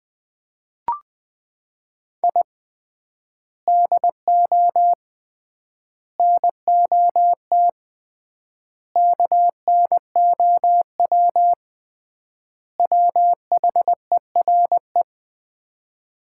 Morse code, heard as a keyed tone, about 20 wpm, 700 Hz. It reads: I DO NOT KNOW WHERE
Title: 3x Word-Spacing - Sets of 5 Words - 20wpm